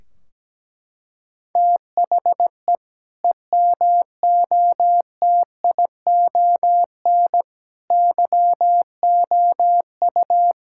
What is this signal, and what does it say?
Morse code, heard as a keyed tone, about 17 wpm, 700 Hz. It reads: THE EMOTION YOU
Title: Sentences from Top 1600 Words 17wpm